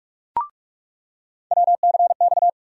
Morse code, heard as a keyed tone, about 47 wpm, 700 Hz. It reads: WCX
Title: USA Call Sign Postfixes - 3 Letters 45wpm